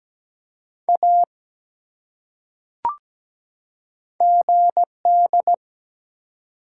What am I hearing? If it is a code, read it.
Morse code, heard as a keyed tone, about 17 wpm, 700 Hz. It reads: A GD
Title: Top 180 QSO Elements 17wpm